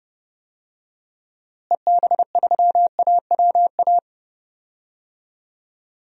Morse code, heard as a keyed tone, about 32 wpm, 700 Hz. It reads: EB3AWA
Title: DXCC-100 Call Signs - 6 Characters - Repeated 2x - 30wpm